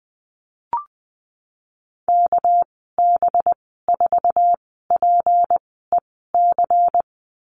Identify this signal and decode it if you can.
Morse code, heard as a keyed tone, about 20 wpm, spaced as for 15 wpm, 700 Hz. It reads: KB4PEC
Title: 213 Lesson 34g - Number 8 - Call Signs 20wpm